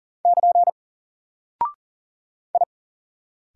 Morse code, heard as a keyed tone, about 39 wpm, 700 Hz. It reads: <KN> I
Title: Single Character (Letter:Number:Punctuation:Prosign) 40wpm